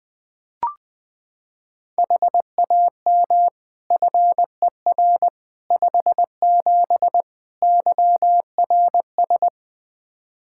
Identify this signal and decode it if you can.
Morse code, heard as a keyed tone, about 21 wpm, 700 Hz. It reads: HAM FER 57 YRS
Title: K1USN SST QSO Practice 20wpm